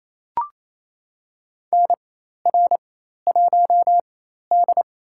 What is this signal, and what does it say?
Morse code, heard as a keyed tone, about 28 wpm, spaced as for 13 wpm, 700 Hz. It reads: NR1D
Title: USA Call Signs - 4 Characters - 4x Character Spacing - 28wpm